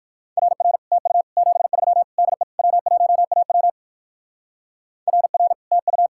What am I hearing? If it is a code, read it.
Morse code, heard as a keyed tone, about 56 wpm, 700 Hz. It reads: RR TU 73 DE W1AW RR TU
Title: Speed-Racing Common QSO Phrases - 45wpm